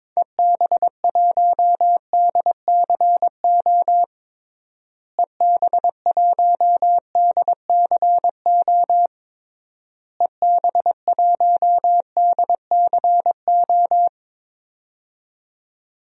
Morse code, heard as a keyed tone, about 22 wpm, 700 Hz. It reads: EB1DCO EB1DCO EB1DCO
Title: DXCC-100 Call Signs - 6 Characters - Repeated 3x - 22wpm